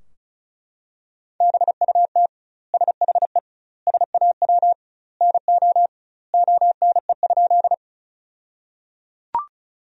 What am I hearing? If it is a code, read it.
Morse code, heard as a keyed tone, about 34 wpm, 700 Hz. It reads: BUT SHE SAW NO ONE?
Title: 2x Word-Spacing - Sentences from Top 300 Words - 35wpm